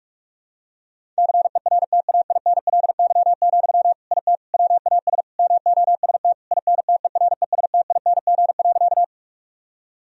Morse code, heard as a keyed tone, about 46 wpm, 700 Hz. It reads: CERTAINLY, IT WAS MOST INTERESTING.
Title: Sentences from Top 1000 Words - Review Part 1 - 45wpm